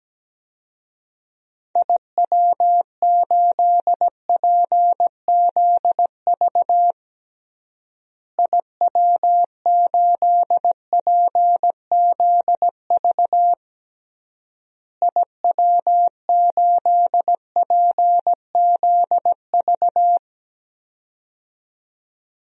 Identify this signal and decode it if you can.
Morse code, heard as a keyed tone, about 17 wpm, 700 Hz. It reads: IW8PZV IW8PZV IW8PZV
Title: DXCC-100 Call Signs - 6 Characters - Repeated 3x - 17wpm